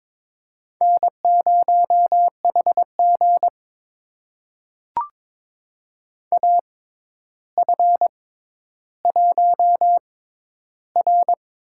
Morse code, heard as a keyed tone, about 23 wpm, spaced as for 11 wpm, 700 Hz. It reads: N0HG AF1R
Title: USA Call Signs - 4 Characters - 6x Character Spacing - 22wpm